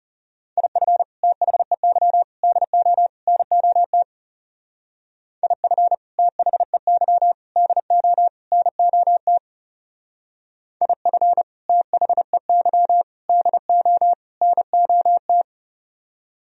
Morse code, heard as a keyed tone, about 36 wpm, 700 Hz. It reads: IF THEY DO NOT IF THEY DO NOT IF THEY DO NOT
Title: Speed-Racing - Sets of 4 Words 30wpm